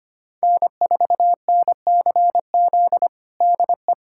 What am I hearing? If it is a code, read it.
Morse code, heard as a keyed tone, about 25 wpm, 700 Hz. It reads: N4NCZ DE
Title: USA Callsign DE Callsign - 5 Characters - 25wpm